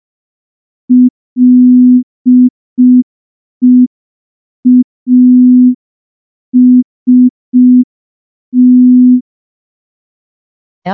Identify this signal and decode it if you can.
Morse code, heard as a keyed tone, about 5 wpm, 250 Hz. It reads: LEAST
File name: morse / morse-training-05WPM